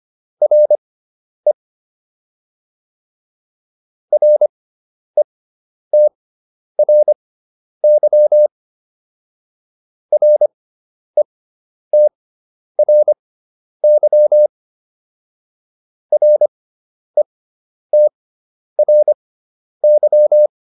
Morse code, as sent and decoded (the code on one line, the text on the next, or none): .-. / .-. - .-. -.-- / .-. - .-. -.-- / .-. - .-. -.--
RE RETRY RETRY RETRY